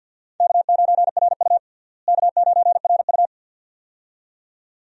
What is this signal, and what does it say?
Morse code, heard as a keyed tone, about 52 wpm, 700 Hz. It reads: K9RU K9RU
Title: USA Call Signs - 4 Characters - Repeated 2x - 50wpm